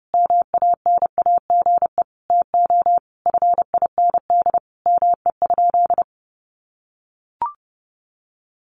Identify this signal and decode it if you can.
Morse code, heard as a keyed tone, about 30 wpm, 700 Hz. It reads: MANAGE TO FIND ME?
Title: Sentences from Top 1400 Words 30wpm